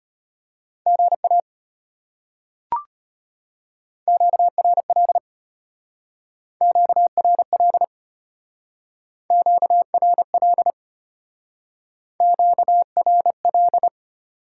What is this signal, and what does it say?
Morse code, heard as a keyed tone, about 33 wpm, 700 Hz. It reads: GA QRL QRL QRL QRL